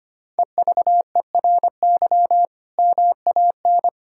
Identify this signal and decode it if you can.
Morse code, heard as a keyed tone, about 25 wpm, 700 Hz. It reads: EVERY MAN
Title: Sentences from Top 1600 Words 25wpm